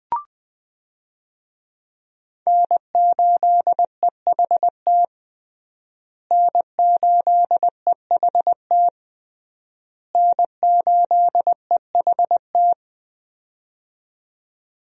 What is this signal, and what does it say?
Morse code, heard as a keyed tone, about 20 wpm, 700 Hz. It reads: N8EHT N8EHT N8EHT
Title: USA Call Signs - 5 Characters - Repeated 3x - 20wpm